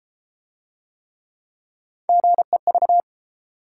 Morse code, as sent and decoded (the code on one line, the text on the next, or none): --. ...-
GEV